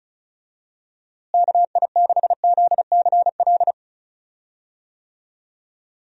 Morse code, heard as a keyed tone, about 34 wpm, 700 Hz. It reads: KI6ZCL
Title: USA Call Signs - 6 Characters - Repeated 2x - 35wpm